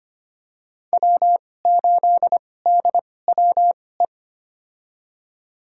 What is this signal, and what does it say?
Morse code, heard as a keyed tone, about 25 wpm, spaced as for 18 wpm, 700 Hz. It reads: W8DWE
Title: USA Call Signs - 5 Characters - 2x Character Spacing - 25wpm